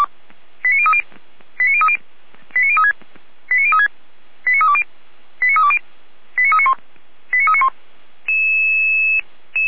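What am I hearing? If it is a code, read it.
Five-tone selective call (ZVEI), heard as quick runs of five tones, tones of about 70 ms. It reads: 89030 89030 89036 89036 89320 89320 89391 89391
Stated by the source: ZVEI five-tone selective call 89504 received on an SDR.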